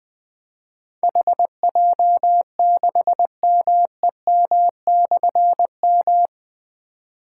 Morse code, heard as a keyed tone, about 20 wpm, 700 Hz. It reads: HJ6MEM/M
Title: Difficult Call Signs 20wpm